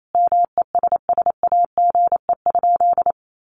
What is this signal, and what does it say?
Morse code, heard as a keyed tone, about 28 wpm, 700 Hz. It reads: MESSAGE?
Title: Sentences from Top 1100 Words 28wpm